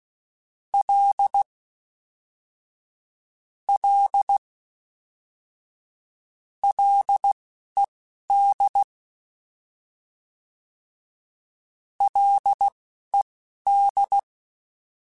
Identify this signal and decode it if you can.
Morse code, heard as a keyed tone, about 16 wpm, spaced as for 11 wpm, 800 Hz. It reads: L L LED LED
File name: F and L / F and L mp3 / F and L Prefixes 3x2 Twice @10